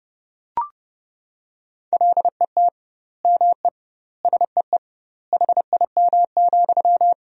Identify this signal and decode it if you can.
Morse code, heard as a keyed tone, about 32 wpm, 700 Hz. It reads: LET ME SEE HIM,
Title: 2x Word-Spacing - Sentences from Top 400 Words - 30wpm